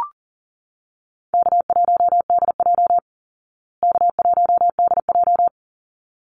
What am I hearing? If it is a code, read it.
Morse code, heard as a keyed tone, about 40 wpm, 700 Hz. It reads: K1DJ K1DJ
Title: Call Signs Repeated 3x - 40wpm